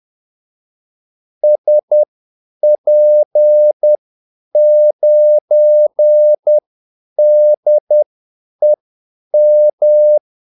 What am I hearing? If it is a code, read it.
Morse code, heard as a keyed tone, about 10 wpm, spaced as for 8 wpm, 600 Hz. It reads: SP9DEM
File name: Callsigns-10-8wpm